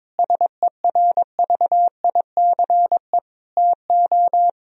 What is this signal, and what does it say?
Morse code, heard as a keyed tone, about 22 wpm, 700 Hz. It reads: SERVICE TO
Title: Sentences from Top 1000 Words - Review Part 4 - 22wpm